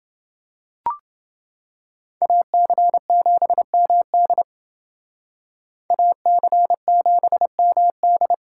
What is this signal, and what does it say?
Morse code, heard as a keyed tone, about 31 wpm, 700 Hz. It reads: AC7MD AC7MD